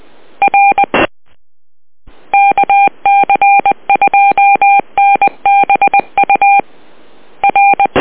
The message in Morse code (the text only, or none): R KC2NBU R